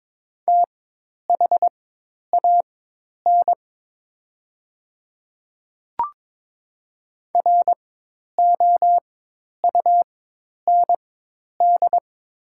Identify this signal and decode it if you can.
Morse code, heard as a keyed tone, about 23 wpm, spaced as for 10 wpm, 700 Hz. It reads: THAN ROUND
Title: 4x Character Spacing - Top 200 Words - 22wpm